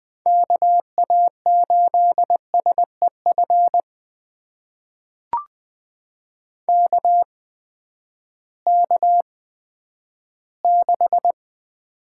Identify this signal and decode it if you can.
Morse code, heard as a keyed tone, about 20 wpm, spaced as for 11 wpm, 700 Hz. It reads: KA8SEF KK6
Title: USA Call Signs - 6 Characters - 8x Character Spacing - 20wpm